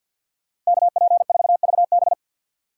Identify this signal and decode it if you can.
Morse code, heard as a keyed tone, about 61 wpm, 700 Hz. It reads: KP4VB
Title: Speed Racing - USA Call Signs 50wpm